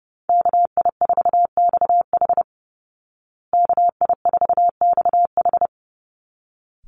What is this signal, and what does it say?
Morse code, heard as a keyed tone, about 30 wpm, 700 Hz. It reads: KI4XH KI4XH
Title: Call Signs Repeated 3x - 30wpm